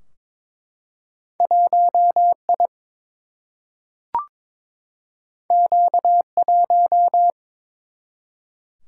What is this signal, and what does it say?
Morse code, heard as a keyed tone, about 23 wpm, 700 Hz. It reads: 1 I Q 1
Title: Single-Letter Single-Number 22wpm